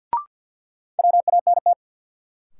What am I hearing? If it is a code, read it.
Morse code, heard as a keyed tone, about 53 wpm, 700 Hz. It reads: WANT
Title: Sets of 2 Words - New Word Drill - 50wpm